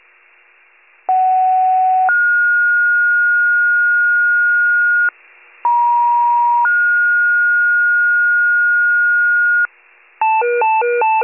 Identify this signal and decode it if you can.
Two-tone page, heard as a one-second tone, then a three-second tone, about 750 then 1430 Hz, about 950 then 1430 Hz. It is heard from 1.1 to 9.7 s.